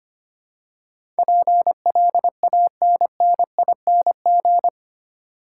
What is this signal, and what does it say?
Morse code, heard as a keyed tone, about 25 wpm, 700 Hz. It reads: PLANNING
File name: 8 Letter Words - First 3 Letters Given - 25wpm